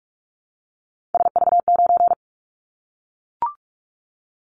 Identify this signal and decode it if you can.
Morse code, heard as a keyed tone, about 44 wpm, 700 Hz. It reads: SV9
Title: Callsign Prefixes for Top 100 DXCC - 45wpm